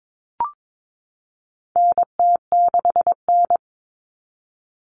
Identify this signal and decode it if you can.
Morse code, heard as a keyed tone, about 22 wpm, 700 Hz. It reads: NT6N
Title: All Cut Numbers - 4-Digit 22wpm